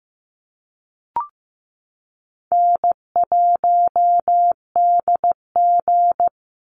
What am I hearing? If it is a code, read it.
Morse code, heard as a keyed tone, about 15 wpm, 700 Hz. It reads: N1DG
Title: Call Signs Repeated 3x - 15wpm